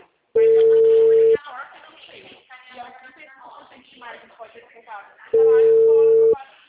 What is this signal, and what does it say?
Ringback tone (China, 450 Hz), with speech.